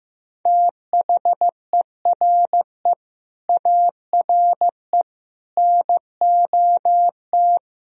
Morse code, heard as a keyed tone, about 16 wpm, 700 Hz. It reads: THERE ARE NOT
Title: Sentences from Top 1000 Words - Review Part 1 - 15wpm